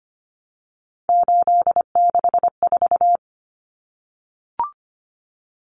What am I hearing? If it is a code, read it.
Morse code, heard as a keyed tone, about 25 wpm, 700 Hz. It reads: 864
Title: Common Cut Numbers - 3-Digit 25wpm